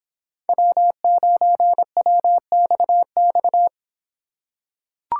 Morse code, heard as a keyed tone, about 24 wpm, 700 Hz. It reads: W9WXX